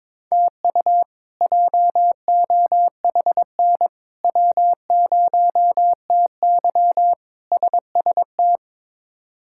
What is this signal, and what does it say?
Morse code, heard as a keyed tone, about 22 wpm, 700 Hz. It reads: TU JOHN W0TY SST
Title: K1USN SST QSO Practice - 22wpm